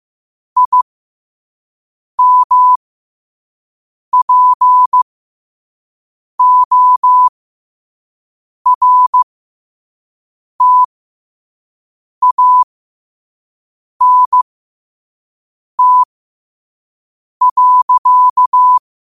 Morse code, heard as a keyed tone, about 15 wpm, spaced as for 5 wpm, 1000 Hz. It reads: IMPORTANT.
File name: qotdfiles / 5wpm